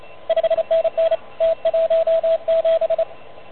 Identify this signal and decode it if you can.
Morse code, heard as a keyed tone, about 30 wpm, 590 Hz, with radio noise. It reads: 5NN T17
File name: truSDX / MWC29.8